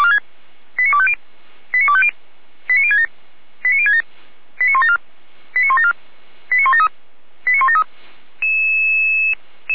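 Five-tone selective call (ZVEI), heard as quick runs of five tones, tones of about 70 ms. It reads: 89250 89250 89067 89067 89174 89174 89173 89173
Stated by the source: ZVEI five-tone call 89240 received by SDR, from the audio server of Feuerwehr Orlamünde.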